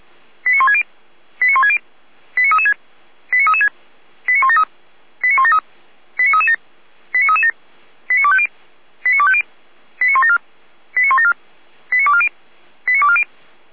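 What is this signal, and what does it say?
Five-tone selective call (ZVEI), heard as quick runs of five tones, tones of about 70 ms. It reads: 89160 89160 89336 89336 89173 89173 89337 89337 89250 89250 89174 89174 89240 89240
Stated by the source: ZVEI five-tone selective call received on SDR, from the Feuerwehr Orlamünde fire brigade archive.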